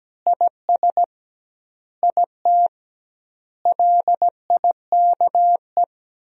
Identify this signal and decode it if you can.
Morse code, heard as a keyed tone, about 17 wpm, 700 Hz. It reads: IS IT LIKE
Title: 2x Word-Spacing - Sentences from Top 500 Words - 17wpm